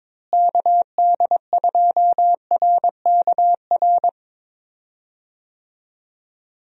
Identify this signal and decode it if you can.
Morse code, heard as a keyed tone, about 22 wpm, 700 Hz. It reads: KD2RKR
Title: USA Call Signs - 6 Characters - Repeated 2x - 22wpm